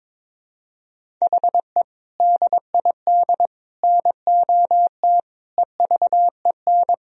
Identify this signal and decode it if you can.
Morse code, heard as a keyed tone, about 22 wpm, 700 Hz. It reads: HE DID NOT EVEN
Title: Sentences from Top 1000 Words - Review Part 1 - 22wpm